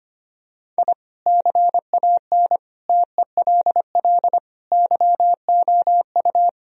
Morse code, heard as a keyed tone, about 25 wpm, 700 Hz. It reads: I CAN TELL YOU
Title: Sentences from Top 1000 Words - Review Part 1 - 25wpm